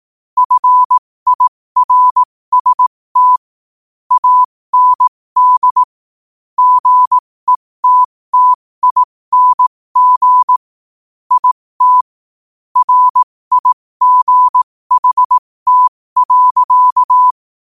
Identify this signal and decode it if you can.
Morse code, heard as a keyed tone, about 18 wpm, spaced as for 15 wpm, 1000 Hz. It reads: FIRST AND GETTING IT RIGHT.
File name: qotdfiles / 15wpm